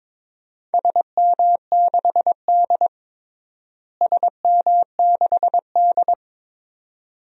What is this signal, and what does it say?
Morse code, heard as a keyed tone, about 22 wpm, 700 Hz. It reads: SM6D SM6D
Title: DXCC-100 Call Signs - 4 Characters - Repeated 2x - 22wpm